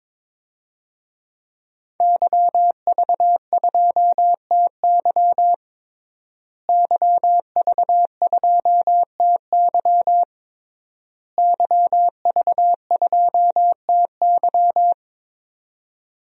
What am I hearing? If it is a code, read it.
Morse code, heard as a keyed tone, about 21 wpm, 700 Hz. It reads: YV2TY YV2TY YV2TY
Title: DXCC-100 Call Signs - 5 Characters - Repeated 3x - 22wpm